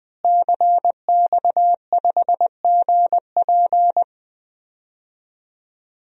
Morse code, heard as a keyed tone, about 20 wpm, 700 Hz. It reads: CX5GP
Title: DXCC-100 Call Signs - 5 Characters - Repeated 2x - 20wpm